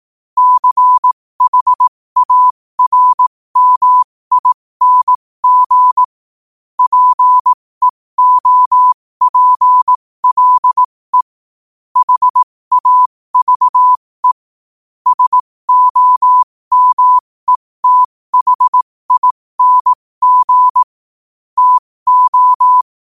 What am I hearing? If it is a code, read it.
Morse code, heard as a keyed tone, about 18 wpm, spaced as for 15 wpm, 1000 Hz. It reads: CHARMING PEOPLE HAVE SOMETHING TO
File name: qotdfiles / 15wpm